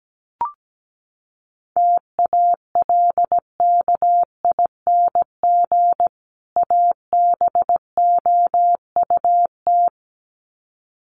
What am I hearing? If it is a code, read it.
Morse code, heard as a keyed tone, about 17 wpm, 700 Hz. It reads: TALKING ABOUT
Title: Sets of 2 Words 17wpm